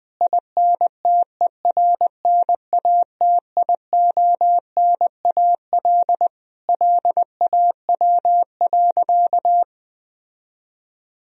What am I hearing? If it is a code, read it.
Morse code, heard as a keyed tone, about 20 wpm, 700 Hz. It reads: INTERNATIONAL LAW.
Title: Sentences from Top 1000 Words - Review Part 4 - 20wpm